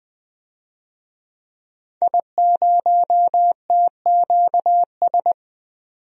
Morse code, heard as a keyed tone, about 20 wpm, 700 Hz. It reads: I0TQS
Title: DXCC-100 Call Signs - 5 Characters - 20wpm